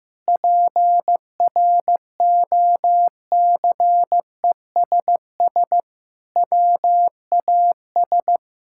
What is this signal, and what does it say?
Morse code, heard as a keyed tone, about 16 wpm, 700 Hz. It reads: PROCESS WAS
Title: Sentences from Top 1000 Words - Review Part 5 - 15wpm